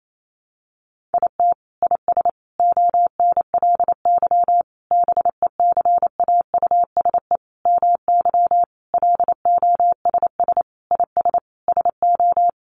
Morse code, heard as a keyed tone, about 28 wpm, 700 Hz. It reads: IT IS ONLY BECAUSE MY LOSS IS SO
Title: Sentences from Top 1400 Words 28wpm